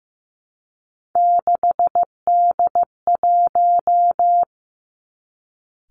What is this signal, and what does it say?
Morse code, heard as a keyed tone, about 15 wpm, 700 Hz. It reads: 6D1
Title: Three Challenging Letter-Number Combinations 15wpm